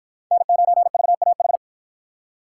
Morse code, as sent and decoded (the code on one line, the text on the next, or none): -. ----. ...- .- ....
N9VAH